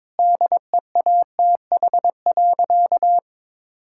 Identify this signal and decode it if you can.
Morse code, heard as a keyed tone, about 22 wpm, 700 Hz. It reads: DEATH.
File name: practice / 22wpm / Sentences with 8-Word Phrases - 22wpm